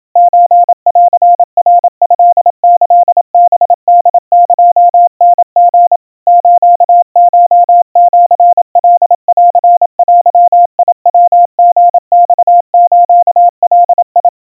Morse code (Japanese Wabun code), heard as a keyed tone, about 27 wpm, 700 Hz. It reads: ソンナトキハホエタリ スコシカンデヤリマスガ
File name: kawa / Morse sample / kamyu 100 cpm 700 Hz Crispy